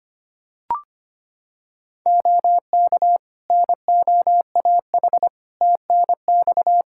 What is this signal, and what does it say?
Morse code, heard as a keyed tone, about 25 wpm, 700 Hz. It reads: OK NOAH TNX
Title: Practice QSO - Exchange 4 - Ending - 25wpm